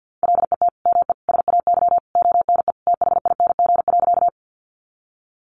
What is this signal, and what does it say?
Morse code, heard as a keyed tone, about 50 wpm, 700 Hz. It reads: LET ME SAY ONE THING.
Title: Sentences from Top 300 Words 50wpm